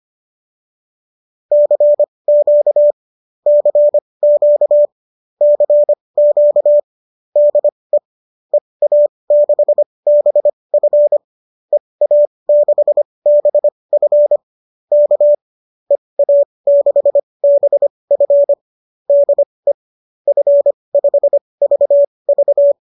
Morse code, heard as a keyed tone, about 25 wpm, spaced as for 20 wpm, 590 Hz. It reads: CQ CQ CQ DE EA6BF EA6BF K EA6BF DE F5VV
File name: qso214 20